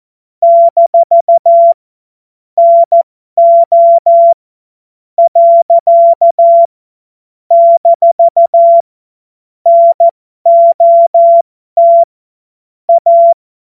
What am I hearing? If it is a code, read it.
Morse code, heard as a keyed tone, about 14 wpm, spaced as for 10 wpm, 680 Hz. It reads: - NO. - NOT A
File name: Morsekurs / Witze